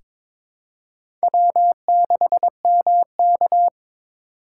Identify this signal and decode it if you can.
Morse code, heard as a keyed tone, about 21 wpm, 700 Hz. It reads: W6MK
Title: USA Call Signs - 4 Characters - 3x Character Spacing - 22wpm